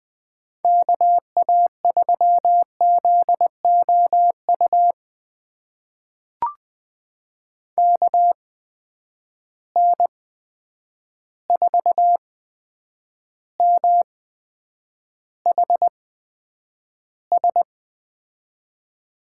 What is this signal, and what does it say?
Morse code, heard as a keyed tone, about 20 wpm, spaced as for 9 wpm, 700 Hz. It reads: KA3ZOU KN4MHS
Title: USA Call Signs - 6 Characters - 8x Character Spacing - 20wpm